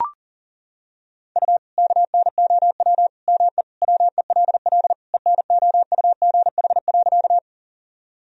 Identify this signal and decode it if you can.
Morse code, heard as a keyed tone, about 40 wpm, 700 Hz. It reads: U KNOW ME WELL ENOUGH.
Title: Sentences from Top 400 Words and Abbreviations 40wpm